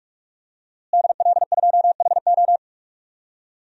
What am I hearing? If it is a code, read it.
Morse code, heard as a keyed tone, about 45 wpm, 700 Hz. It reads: DL2HO
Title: DXCC-100 Call Signs - 5 Characters - Repeated 2x - 45wpm